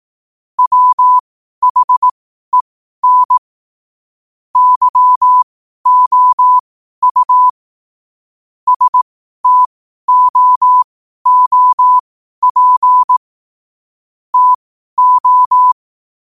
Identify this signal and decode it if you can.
Morse code, heard as a keyed tone, about 18 wpm, spaced as for 12 wpm, 1000 Hz. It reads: WHEN YOU STOOP TO
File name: qotdfiles / 12wpm